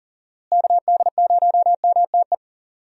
Morse code, heard as a keyed tone, about 40 wpm, 700 Hz. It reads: KD0MTE
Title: USA Call Signs - 6 Characters - Repeated 2x - 40wpm